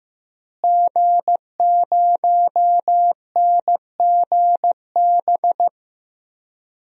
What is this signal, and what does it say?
Morse code, heard as a keyed tone, about 15 wpm, 700 Hz. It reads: G0NGB
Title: DXCC-100 Call Signs - 5 Characters - Repeated 3x - 15wpm